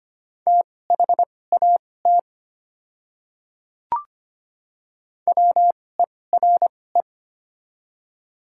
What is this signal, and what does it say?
Morse code, heard as a keyed tone, about 25 wpm, spaced as for 18 wpm, 700 Hz. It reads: THAT WERE
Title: 2x Character Spacing - Top 100 Words - 25wpm